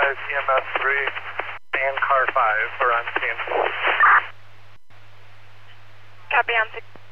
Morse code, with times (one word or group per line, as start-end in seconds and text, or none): none